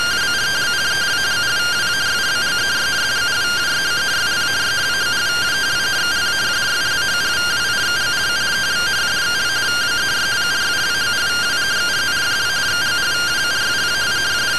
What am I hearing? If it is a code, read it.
Two-tone teleprinter signal, about 170 Hz shift.